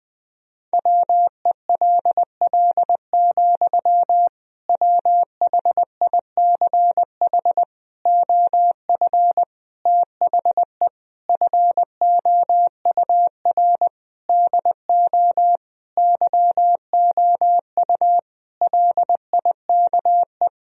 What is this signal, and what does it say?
Morse code, heard as a keyed tone, about 21 wpm, 700 Hz. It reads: WELL, WHICH OF THE FOUR DO YOU LIKE
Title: Sentences from Top 1000 Words - Review Part 1 - 20wpm